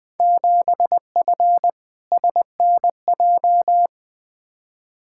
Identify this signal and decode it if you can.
Morse code, heard as a keyed tone, about 20 wpm, 700 Hz. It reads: 7F SNJ
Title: Field Day Exchanges - 20wpm